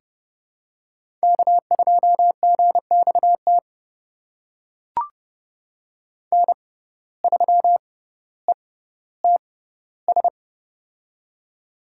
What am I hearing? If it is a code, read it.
Morse code, heard as a keyed tone, about 30 wpm, spaced as for 15 wpm, 700 Hz. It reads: K2GXT N3ETS
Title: USA Call Signs - 5 Characters - 6x Character Spacing - 30wpm